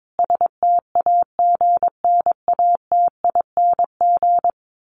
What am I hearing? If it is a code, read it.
Morse code, heard as a keyed tone, about 22 wpm, 700 Hz. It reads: STAGNATING
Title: A Sound Of Thunder 22wpm